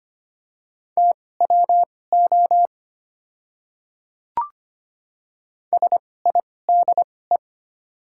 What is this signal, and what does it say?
Morse code, heard as a keyed tone, about 25 wpm, spaced as for 18 wpm, 700 Hz. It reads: TWO SIDE
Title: 2x Character Spacing - Top 100 Words - 25wpm